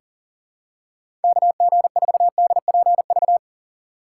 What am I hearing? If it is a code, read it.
Morse code, heard as a keyed tone, about 40 wpm, 700 Hz. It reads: KG4DPV
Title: USA Call Signs - 6 Characters - 6x Character Spacing - 40wpm